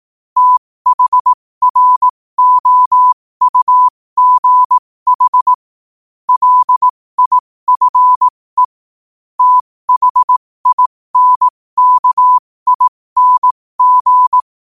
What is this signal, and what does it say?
Morse code, heard as a keyed tone, about 18 wpm, spaced as for 15 wpm, 1000 Hz. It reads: THROUGH LIFE THINKING
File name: qotdfiles / 15wpm